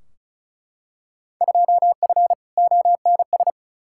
Morse code, heard as a keyed tone, about 34 wpm, 700 Hz. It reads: 2F ONS
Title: Field Day Exchanges - 35wpm